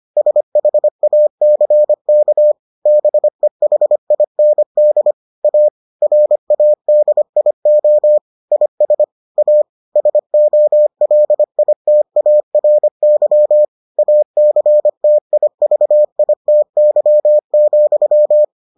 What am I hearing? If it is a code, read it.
Morse code, heard as a keyed tone, about 25 wpm, 600 Hz. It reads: SHACK BEHIND A RADIO IS A SOLITARY ACTIVITY,